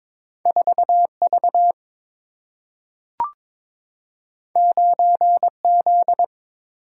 Morse code, heard as a keyed tone, about 22 wpm, 690 Hz. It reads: 4V 9Z
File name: practice / 22wpm / Two Letter-Number Combinations - 22wpm